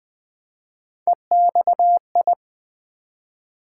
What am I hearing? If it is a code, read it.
Morse code, heard as a keyed tone, about 20 wpm, 700 Hz. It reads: EXI
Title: Build A Word - 1 Letter At A Time - 3 to 5-Letter Words - 20wpm